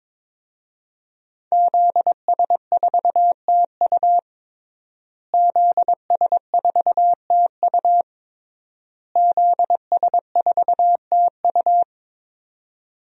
Morse code, heard as a keyed tone, about 21 wpm, 700 Hz. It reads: ZS4TU ZS4TU ZS4TU
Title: DXCC-100 Call Signs - 5 Characters - Repeated 3x - 22wpm